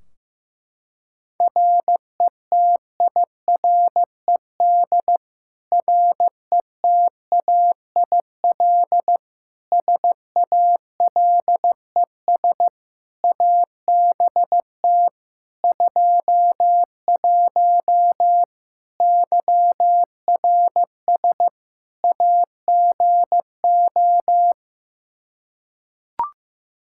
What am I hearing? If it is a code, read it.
Morse code, heard as a keyed tone, about 16 wpm, 700 Hz. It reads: RETIRED RETAIL SALES ABT 21 YRS AGO
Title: Practice QSO - Exchange 3 - More Info - 15wpm